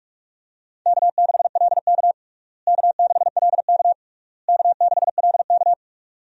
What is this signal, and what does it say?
Morse code, heard as a keyed tone, about 41 wpm, 700 Hz. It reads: K6LK K6LK K6LK
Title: USA Call Signs - 4 Characters - Repeated 3x - 45wpm